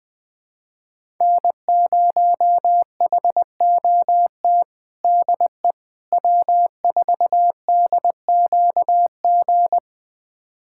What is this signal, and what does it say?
Morse code, heard as a keyed tone, about 20 wpm, 700 Hz. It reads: N0HOT DE W4DQG
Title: USA Callsign DE Callsign - 5 Characters - 20wpm